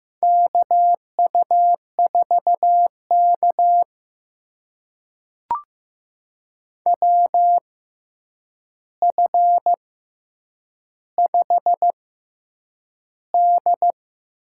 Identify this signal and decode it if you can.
Morse code, heard as a keyed tone, about 15 wpm, spaced as for 8 wpm, 700 Hz. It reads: KU4K WF5D
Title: USA Call Signs - 4 Characters - 6x Character Spacing - 15wpm